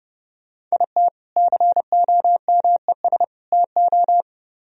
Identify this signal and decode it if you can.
Morse code, heard as a keyed tone, about 37 wpm, 700 Hz. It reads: IT COMES TO
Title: Speed-Racing - Sets of 3 Words 30wpm